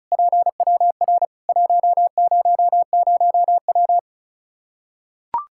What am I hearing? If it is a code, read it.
Morse code, heard as a keyed tone, about 36 wpm, 700 Hz. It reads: PWR 100W
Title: Practice QSO - Exchange 2 - Info - 35wpm